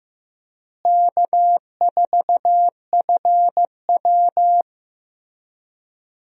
Morse code, heard as a keyed tone, about 15 wpm, 700 Hz. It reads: K4FW
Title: USA Call Signs - 4 Characters - Repeated 2x - 15wpm